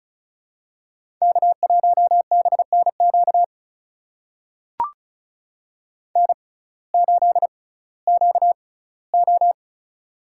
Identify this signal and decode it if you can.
Morse code, heard as a keyed tone, about 36 wpm, spaced as for 19 wpm, 700 Hz. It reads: K1BNQ N8QO
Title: USA Call Signs - 6x Character Spacing - 35wpm